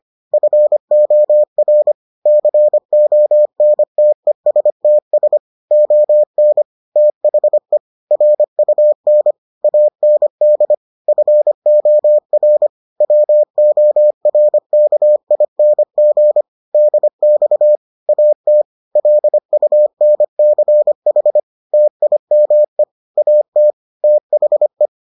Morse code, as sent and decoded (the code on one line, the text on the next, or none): ..-. --- .-. / -.-. --- -. - . ... - ... / --- -. / - .... / .-. ..- -. / .- -. -.. / ..-. --- .-. / .-- --- .-. -.- .. -. --. / -.. -..- / .- - / .-.. ..- -. -.-. .... / - .. -- . / .- - / - .... .
FOR CONTESTS ON THE RUN AND FOR WORKING DX AT LUNCH TIME AT THE